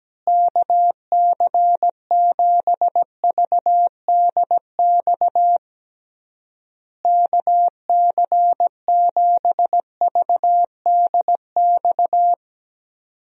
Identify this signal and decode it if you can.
Morse code, heard as a keyed tone, about 17 wpm, 700 Hz. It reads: KC7VDX KC7VDX
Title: USA Call Signs - 6 Characters - Repeated 3x - 17wpm